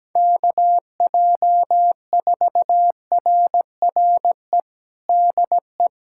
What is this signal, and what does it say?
Morse code, heard as a keyed tone, about 17 wpm, 700 Hz. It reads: KJ4RRE DE
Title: USA Callsign DE Callsign - 6 Characters - 17wpm